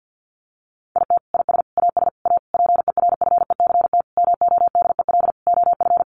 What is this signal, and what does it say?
Morse code, heard as a keyed tone, about 50 wpm, 700 Hz. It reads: IT IS AS A PERFECT MODEL OF
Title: Sentences from Top 1000 Words 50wpm